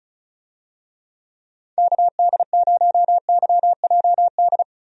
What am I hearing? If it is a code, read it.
Morse code, heard as a keyed tone, about 36 wpm, 700 Hz. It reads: KD0YJD
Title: USA Call Signs - 6 Characters - 35wpm